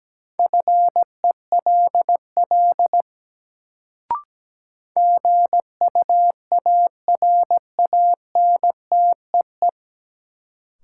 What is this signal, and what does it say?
Morse code, heard as a keyed tone, about 17 wpm, 700 Hz. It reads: FELL GUARANTEE
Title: Sentences from Top 1600 Words - New Word Drill - 17wpm